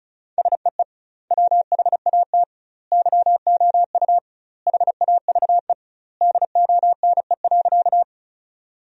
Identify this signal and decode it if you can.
Morse code, heard as a keyed tone, about 34 wpm, 700 Hz. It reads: SEE WHAT YOU HAVE DONE.
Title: 2x Word-Spacing - Sentences from Top 500 Words - 35wpm